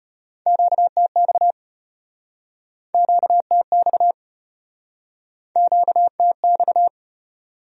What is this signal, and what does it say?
Morse code, heard as a keyed tone, about 37 wpm, 700 Hz. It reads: QTX QTX QTX